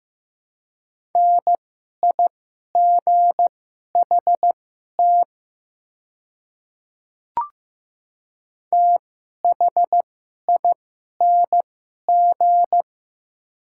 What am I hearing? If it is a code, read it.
Morse code, heard as a keyed tone, about 15 wpm, spaced as for 11 wpm, 700 Hz. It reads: NIGHT THING
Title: 2x Character Spacing - Top 300 Words - 15wpm